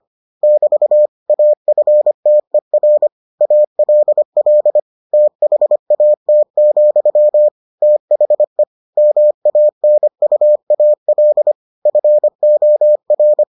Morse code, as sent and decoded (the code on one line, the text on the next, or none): -...- / .- ..-. - . .-. / .- .-.. .-.. / - .... .- - --..-- / - .... / -- .- -. ..- .- .-.. / ..-. --- .-.
<BT> AFTER ALL THAT, THE MANUAL FOR